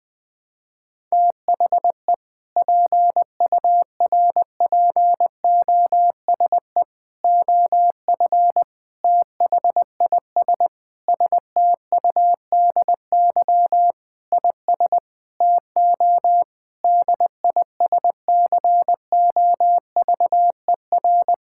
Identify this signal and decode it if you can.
Morse code, heard as a keyed tone, about 20 wpm, 700 Hz. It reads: THE PURPOSE OF THIS STUDY IS TO DISCOVER
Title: Sentences with 5-Word Phrases - 20wpm